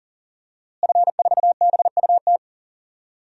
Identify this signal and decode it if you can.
Morse code, heard as a keyed tone, about 37 wpm, 700 Hz. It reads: F4BUT
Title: DXCC-100 Call Signs - 5 Characters - Repeated 3x - 40wpm